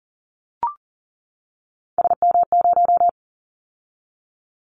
Morse code, heard as a keyed tone, about 39 wpm, 700 Hz. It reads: SM0
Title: Callsign Prefixes for Top 25 DXCC - 40wpm